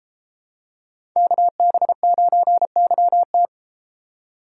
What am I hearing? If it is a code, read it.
Morse code, heard as a keyed tone, about 31 wpm, 700 Hz. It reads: KB9YT